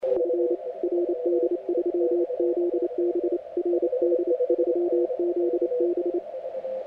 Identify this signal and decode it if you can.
Morse code, heard as a keyed tone, about 28 wpm, 350 Hz, with radio noise. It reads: F RD3ZB RD3ZB
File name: audio mp3 files / A71BX 40m CW 2008-4-12 2-0-53